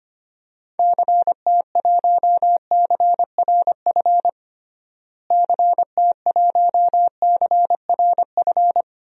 Morse code, heard as a keyed tone, about 25 wpm, 700 Hz. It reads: CT1CRF CT1CRF
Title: DXCC-100 Call Signs - 6 Characters - Repeated 2x - 25wpm